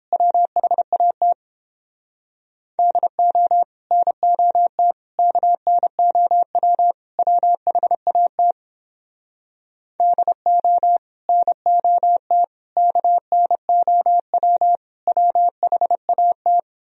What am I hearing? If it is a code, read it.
Morse code, heard as a keyed tone, about 31 wpm, 700 Hz. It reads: WHAT DO NOT KNOW WHAT DO NOT KNOW WHAT